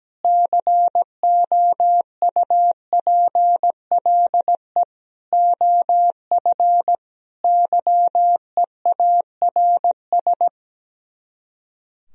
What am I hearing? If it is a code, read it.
Morse code, heard as a keyed tone, about 17 wpm, 700 Hz. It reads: COUPLE OF YEARS
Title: Speed-Racing - Sets of 5 Words 17wpm